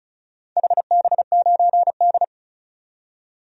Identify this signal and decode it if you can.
Morse code, heard as a keyed tone, about 36 wpm, 700 Hz. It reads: HB9D
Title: DXCC-100 Call Signs - 4 Characters - 35wpm